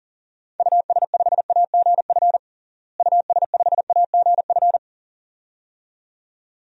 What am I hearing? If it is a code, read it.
Morse code, heard as a keyed tone, about 40 wpm, 700 Hz. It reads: US5AGF US5AGF
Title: DXCC-100 Call Signs - 6 Characters - Repeated 2x - 40wpm